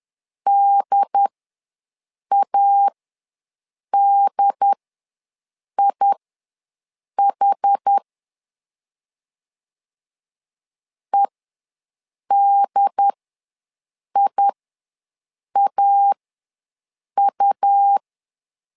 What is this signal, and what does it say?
Morse code, heard as a keyed tone, about 10 wpm, spaced as for 6 wpm, 800 Hz. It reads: DADIH EDIAU
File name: LEZIONI / 2 UHDN / 2 UHD ws 10wpm 2058